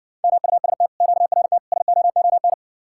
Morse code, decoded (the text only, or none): MUST QRT SOON